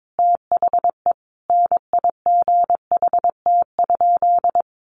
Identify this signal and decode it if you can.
Morse code, heard as a keyed tone, about 22 wpm, 700 Hz. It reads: THE NIGHT?
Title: Sentences from Top 1300 Words 22wpm